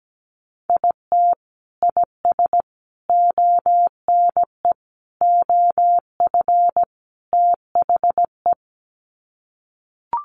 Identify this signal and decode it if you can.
Morse code, heard as a keyed tone, about 17 wpm, 700 Hz. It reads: IT IS ONE OF THE
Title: Sets of 5 Words 17wpm